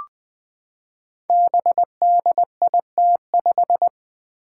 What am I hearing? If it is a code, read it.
Morse code, heard as a keyed tone, about 20 wpm, 700 Hz. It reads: BDIT5
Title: Five Challenging Letter-Number Combinations 20wpm